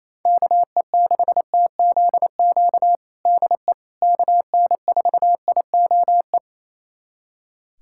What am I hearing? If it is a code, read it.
Morse code, heard as a keyed tone, about 27 wpm, 700 Hz. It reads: KE6TZQ DE KN4IOE
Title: USA Callsign DE Callsign 28wpm